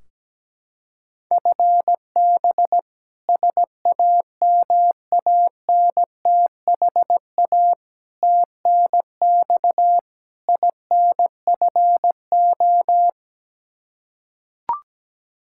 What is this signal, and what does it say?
Morse code, heard as a keyed tone, about 17 wpm, 700 Hz. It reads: FB SAMANTHA TNX INFO
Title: K1USN SST QSO Practice 17wpm